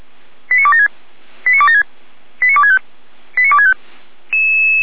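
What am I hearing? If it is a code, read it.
Five-tone selective call (ZVEI), heard as quick runs of five tones, tones of about 70 ms. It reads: 89276 89276 89265 89265
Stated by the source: Five-tone ZVEI call received on an SDR, fire brigade Feuerwehr Orlamünde.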